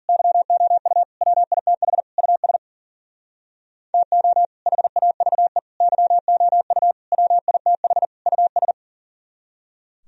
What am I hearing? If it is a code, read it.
Morse code, heard as a keyed tone, about 51 wpm, 700 Hz. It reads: YOU WITH US TO HAVE YOU WITH US